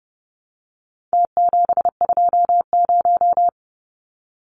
Morse code, heard as a keyed tone, about 30 wpm, 700 Hz. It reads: T720
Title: Numbers - Contest Style - 4-Digit 30wpm